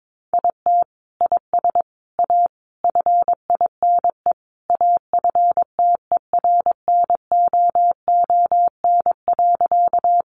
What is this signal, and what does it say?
Morse code, heard as a keyed tone, about 22 wpm, 700 Hz. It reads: IT IS A FINE AFTERNOON.
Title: Sentences from Top 1500 Words 22wpm